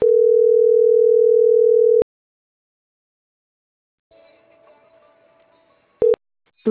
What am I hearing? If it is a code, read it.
Ringback tone (North America), 440 + 480 Hz.
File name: CENTRAL NMP / out marianacruznmp